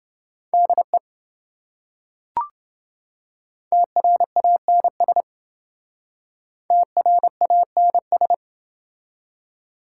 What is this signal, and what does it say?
Morse code, heard as a keyed tone, about 28 wpm, 700 Hz. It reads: DE TRANS TRANS